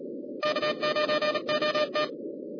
Morse code, heard as a keyed tone, about 38 wpm, 1190 Hz, with noise. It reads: K9OT
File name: 2011 bcc-challenge sonntag tape